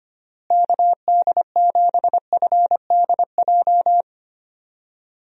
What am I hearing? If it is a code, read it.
Morse code, heard as a keyed tone, about 25 wpm, 700 Hz. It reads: KD7FDJ
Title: USA Call Signs - 6 Characters - Repeated 3x - 25wpm